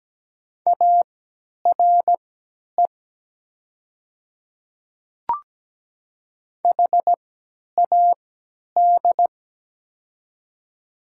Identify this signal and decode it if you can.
Morse code, heard as a keyed tone, about 17 wpm, spaced as for 10 wpm, 700 Hz. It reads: ARE HAD
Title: 3x Character Spacing - Top 100 Words - 17wpm